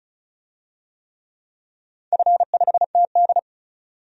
Morse code, heard as a keyed tone, about 36 wpm, 700 Hz. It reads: F5TD
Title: DXCC-100 Call Signs - 4 Characters - 35wpm